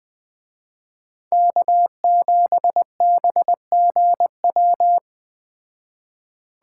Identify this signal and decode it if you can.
Morse code, heard as a keyed tone, about 19 wpm, 700 Hz. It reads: K7BGW
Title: Speed Racing - USA 1x3 Call Signs 17wpm